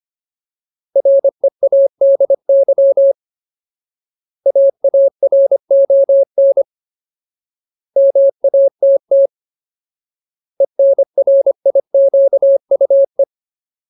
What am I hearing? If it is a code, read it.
Morse code, heard as a keyed tone, about 25 wpm, 550 Hz. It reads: READY AARON MATT ENRIQUE
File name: Ultimate Word List / 25WPM Ultimate